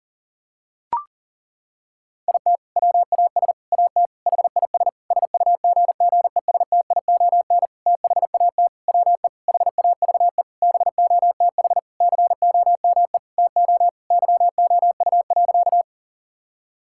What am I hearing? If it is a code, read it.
Morse code, heard as a keyed tone, about 40 wpm, 700 Hz. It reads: IT WAS AT HIS SUGGESTION THAT WE HAVE BOTH COME TO YOU.
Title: Sentences from Top 1600 Words 40wpm